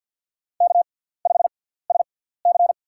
Morse code, heard as a keyed tone, about 53 wpm, spaced as for 17 wpm, 700 Hz. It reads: K5SC
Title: USA Call Signs - 4 Characters - 6x Character Spacing - 50wpm